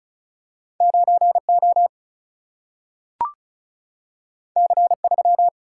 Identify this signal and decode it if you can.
Morse code, heard as a keyed tone, about 36 wpm, 700 Hz. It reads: 9O C3
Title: Single-Letter Single-Number 35wpm